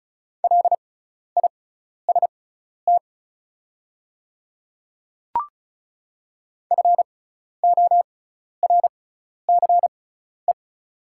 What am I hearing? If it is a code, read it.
Morse code, heard as a keyed tone, about 36 wpm, spaced as for 12 wpm, 700 Hz. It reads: LIST FORCE
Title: 6x Character Spacing - Top 500 Words - 35wpm